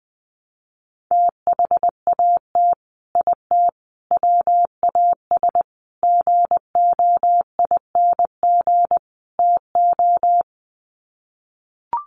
Morse code, heard as a keyed tone, about 20 wpm, 700 Hz. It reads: THAT IT WAS GOING TO
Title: Sets of 5 Words Encore2 20wpm